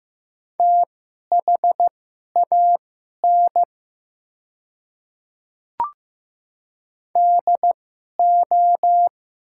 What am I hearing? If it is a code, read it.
Morse code, heard as a keyed tone, about 15 wpm, spaced as for 11 wpm, 700 Hz. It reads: THAN DO
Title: 2x Character Spacing - Top 100 Words - 15wpm